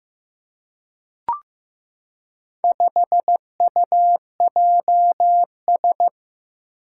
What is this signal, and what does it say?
Morse code, heard as a keyed tone, about 15 wpm, 700 Hz. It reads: 5UJS
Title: Four Challenging Letter-Number Combinations 15wpm